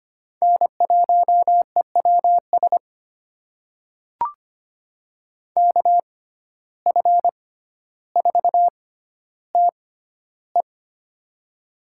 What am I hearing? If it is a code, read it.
Morse code, heard as a keyed tone, about 25 wpm, spaced as for 13 wpm, 700 Hz. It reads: N1EWS KF4TE
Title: USA Call Signs - 5 Characters - 6x Character Spacing - 25wpm